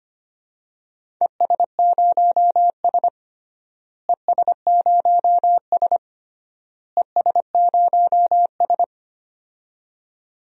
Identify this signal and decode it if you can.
Morse code, heard as a keyed tone, about 25 wpm, 700 Hz. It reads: ES0S ES0S ES0S
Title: DXCC-100 Call Signs - 4 Characters - Repeated 3x - 25wpm